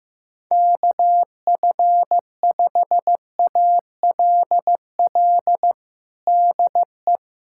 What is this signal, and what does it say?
Morse code, heard as a keyed tone, about 15 wpm, 700 Hz. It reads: KF5ALL DE
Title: USA Callsign DE Callsign 15wpm